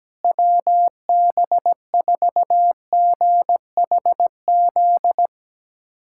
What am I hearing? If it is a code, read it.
Morse code, heard as a keyed tone, about 17 wpm, 700 Hz. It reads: WB4GHZ
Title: K1USN SST QSO Practice - 17wpm